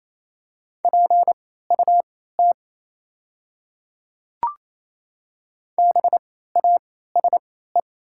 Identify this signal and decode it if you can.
Morse code, heard as a keyed tone, about 27 wpm, spaced as for 16 wpm, 700 Hz. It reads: PUT BASE
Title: 3x Character Spacing - Top 300 Words - 28wpm